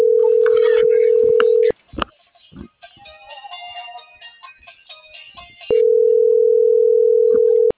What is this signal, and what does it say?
Ringback tone (North America), 440 + 480 Hz.